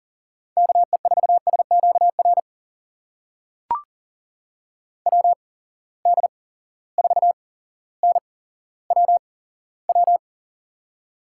Morse code, heard as a keyed tone, about 39 wpm, spaced as for 17 wpm, 700 Hz. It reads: KE4SQR WD4NWW
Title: USA Call Signs - 6 Characters - 8x Character Spacing - 40wpm